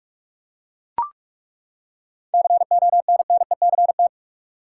Morse code, heard as a keyed tone, about 42 wpm, 700 Hz. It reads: CONNECT
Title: Seven Letter Words - 45wpm